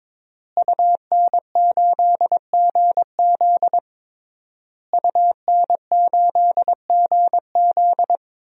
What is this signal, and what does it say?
Morse code, heard as a keyed tone, about 22 wpm, 700 Hz. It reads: UN8GZ UN8GZ
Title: DXCC-100 Call Signs - 5 Characters - Repeated 3x - 22wpm